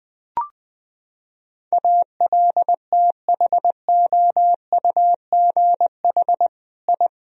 Morse code, heard as a keyed tone, about 20 wpm, 700 Hz. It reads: ALTHOUGH I
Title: Sentences from Top 1000 Words - Review Part 3 - 20wpm